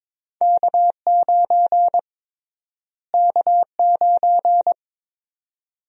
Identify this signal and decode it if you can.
Morse code, heard as a keyed tone, about 22 wpm, 700 Hz. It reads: K9 K9
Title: USA Call Sign Prefixes - Repeated 2x - 22wpm